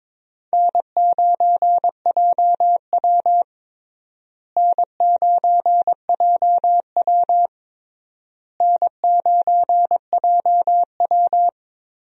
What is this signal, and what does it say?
Morse code, heard as a keyed tone, about 22 wpm, 700 Hz. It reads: N9JW N9JW N9JW
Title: USA Call Signs - 4 Characters - Repeated 3x - 22wpm